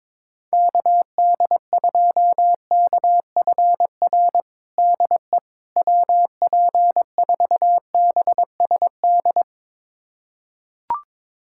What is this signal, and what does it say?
Morse code, heard as a keyed tone, about 21 wpm, 700 Hz. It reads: KD2KFR DE WP4BSD
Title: USA Callsign DE Callsign 22wpm